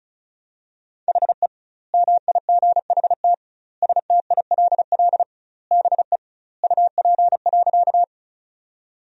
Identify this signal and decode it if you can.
Morse code, heard as a keyed tone, about 36 wpm, 700 Hz. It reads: HE MIGHT STILL BE UP.
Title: 2x Word-Spacing - Sentences from Top 300 Words - 35wpm